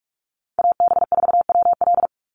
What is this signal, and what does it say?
Morse code, heard as a keyed tone, about 44 wpm, 700 Hz. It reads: AB4WL
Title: NA SOTA Chaser Callsigns 45wpm